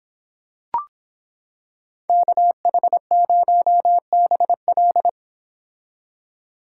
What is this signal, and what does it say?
Morse code, heard as a keyed tone, about 26 wpm, 700 Hz. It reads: KH0BL